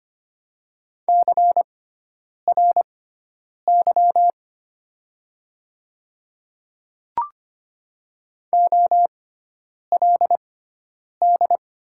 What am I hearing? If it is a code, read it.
Morse code, heard as a keyed tone, about 25 wpm, spaced as for 9 wpm, 700 Hz. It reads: CRY OLD
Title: 6x Character Spacing - Top 500 Words - 25wpm